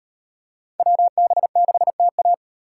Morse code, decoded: WB6TA

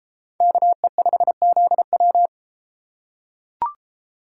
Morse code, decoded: KE5ZW

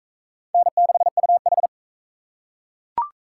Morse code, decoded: N6UH

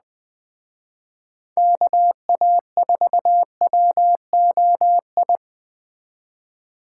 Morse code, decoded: KA4WOI

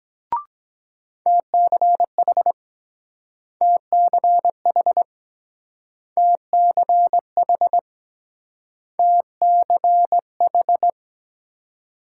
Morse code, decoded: TCH TCH TCH TCH

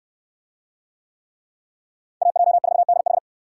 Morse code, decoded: AJ4DH

